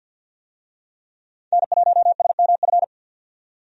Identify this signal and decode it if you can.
Morse code, heard as a keyed tone, about 50 wpm, 700 Hz. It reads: N1SMF
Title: USA Call Signs - 5 Characters - 50wpm